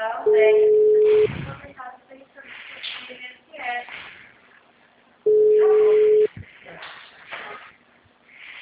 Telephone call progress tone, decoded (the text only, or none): ringback tone